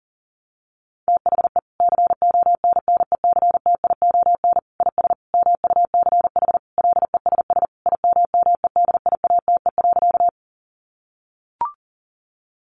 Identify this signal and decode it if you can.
Morse code, heard as a keyed tone, about 37 wpm, 700 Hz. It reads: THE CONNECTION IS MUCH LESS IMMEDIATE.
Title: Sentences from Top 1300 Words 40wpm